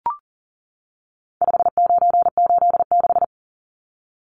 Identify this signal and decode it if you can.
Morse code, heard as a keyed tone, about 40 wpm, 700 Hz. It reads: 5986
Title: Numbers - 4-Digit 40wpm